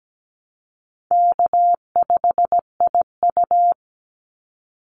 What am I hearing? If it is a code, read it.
Morse code, heard as a keyed tone, about 17 wpm, 700 Hz. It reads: K5IU
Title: Easy Call Signs Repeated 2x - 17wpm